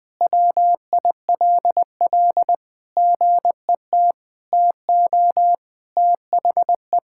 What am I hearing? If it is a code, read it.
Morse code, heard as a keyed tone, about 20 wpm, 700 Hz. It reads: WILL GET TO THE
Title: Sentences from Top 1000 Words - Review Part 6 - 20wpm